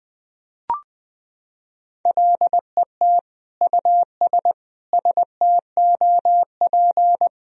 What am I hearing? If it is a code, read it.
Morse code, heard as a keyed tone, about 20 wpm, 700 Hz. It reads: LET US STOP
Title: Sentences from Top 1000 Words - Review Part 1 - 20wpm